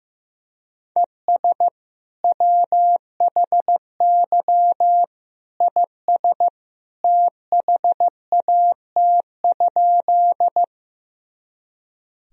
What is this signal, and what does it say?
Morse code, heard as a keyed tone, about 15 wpm, 700 Hz. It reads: ES WHY IS THAT?
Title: Sentences from Top 200 Words and Abbreviations 15wpm